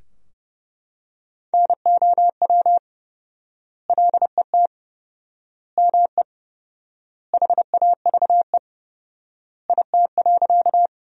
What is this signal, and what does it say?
Morse code, heard as a keyed tone, about 30 wpm, 700 Hz. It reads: NOW LET ME HAVE IT.
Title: 4x Word-Spacing - Sentences from Top 300 Words - 30wpm